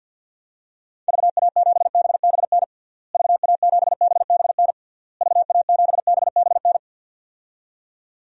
Morse code, heard as a keyed tone, about 47 wpm, 700 Hz. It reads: VA7BBN VA7BBN VA7BBN
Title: DXCC-100 Call Signs - 6 Characters - Repeated 3x - 50wpm